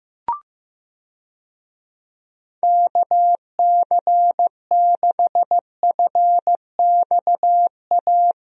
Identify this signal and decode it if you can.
Morse code, heard as a keyed tone, about 16 wpm, 700 Hz. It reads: KC6FXA
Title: USA Call Signs - 6 Characters - Repeated 2x - 15wpm